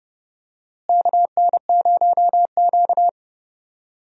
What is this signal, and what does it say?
Morse code, heard as a keyed tone, about 30 wpm, 700 Hz. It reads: KN0Q
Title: USA Call Signs - 4 Characters - Repeated 3x - 30wpm